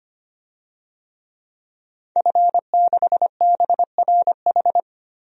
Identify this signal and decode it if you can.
Morse code, heard as a keyed tone, about 25 wpm, 700 Hz. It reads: F6BRH
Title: DXCC-100 Call Signs - 5 Characters - 25wpm